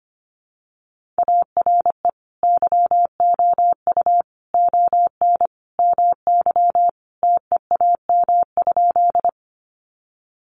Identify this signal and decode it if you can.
Morse code, heard as a keyed tone, about 25 wpm, 700 Hz. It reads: ARE YOU ON MY TEAM?
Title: Sentences from Top 1500 Words 25wpm